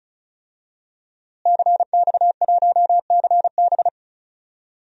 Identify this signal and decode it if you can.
Morse code, heard as a keyed tone, about 36 wpm, 700 Hz. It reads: CX1CB
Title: DXCC-100 Call Signs - 5 Characters - Repeated 3x - 35wpm